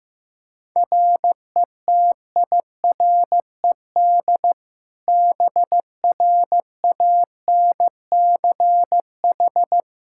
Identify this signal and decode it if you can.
Morse code, heard as a keyed tone, about 15 wpm, 700 Hz. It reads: RETIRED BRANCH
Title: Practice QSO - Exchange 3 - More Info - 15wpm